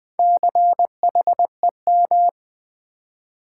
Morse code, heard as a keyed tone, about 20 wpm, 700 Hz. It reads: CHEM
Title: Build A Word - 4 Letters At A Time - 8-Letter Words - 20wpm